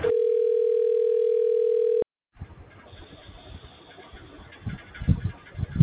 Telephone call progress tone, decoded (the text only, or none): ringback tone